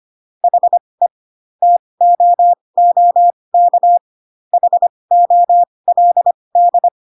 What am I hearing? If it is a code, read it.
Morse code, heard as a keyed tone, about 25 wpm, spaced as for 20 wpm, 700 Hz. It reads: HE TOOK HOLD